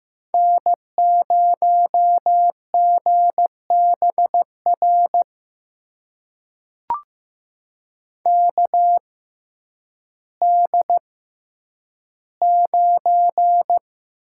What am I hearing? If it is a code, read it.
Morse code, heard as a keyed tone, about 15 wpm, spaced as for 9 wpm, 700 Hz. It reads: N0GBR KD9
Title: USA Call Signs - 6x Character Spacing - 15wpm